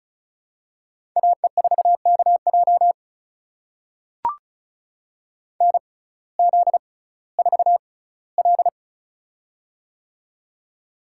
Morse code, heard as a keyed tone, about 36 wpm, spaced as for 19 wpm, 700 Hz. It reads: AE4KJ NZ4L
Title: USA Call Signs - 6x Character Spacing - 35wpm